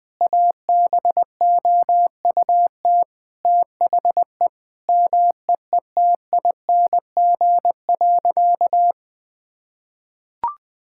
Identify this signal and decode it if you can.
Morse code, heard as a keyed tone, about 19 wpm, 700 Hz. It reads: ABOUT THE MEETING.
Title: Sentences with 5-Word Phrases - Spoken Beginning - 20wpm